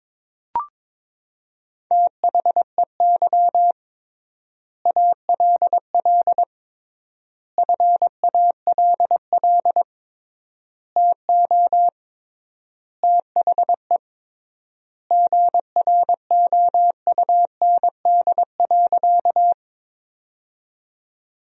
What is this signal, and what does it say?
Morse code, heard as a keyed tone, about 22 wpm, 700 Hz. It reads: THEY ALL FALL TO THE GROUND.
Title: 3x Word-Spacing - Sentences from Top 500 Words - 22wpm